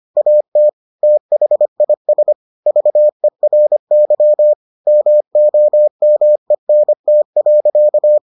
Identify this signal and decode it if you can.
Morse code, heard as a keyed tone, about 25 wpm, 600 Hz. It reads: AT THIS VERY MOMENT.